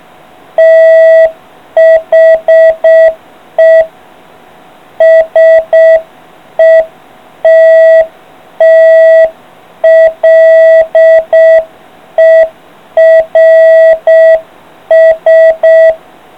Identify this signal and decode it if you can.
Morse code, heard as a keyed tone, about 6 wpm, 660 Hz. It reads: THE SETTLERS